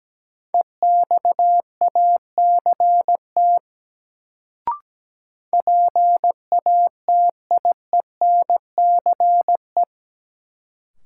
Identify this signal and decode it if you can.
Morse code, heard as a keyed tone, about 17 wpm, 700 Hz. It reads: EXACT PATIENCE
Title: Sentences from Top 1300 Words - New Word Drill - 17wpm